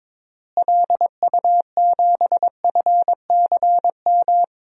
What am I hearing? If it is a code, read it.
Morse code, heard as a keyed tone, about 22 wpm, 700 Hz. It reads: LU7FCM
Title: DXCC-100 Call Signs - 6 Characters - Repeated 2x - 22wpm